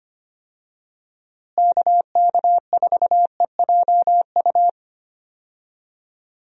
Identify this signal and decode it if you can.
Morse code, heard as a keyed tone, about 25 wpm, 700 Hz. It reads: KK4EJU
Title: USA Call Signs - 6 Characters - Repeated 3x - 25wpm